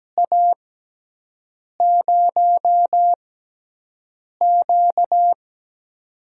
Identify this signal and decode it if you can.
Morse code, heard as a keyed tone, about 17 wpm, spaced as for 6 wpm, 700 Hz. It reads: A0Q
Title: USA Call Signs - 4 Characters - 6x Character Spacing - 17wpm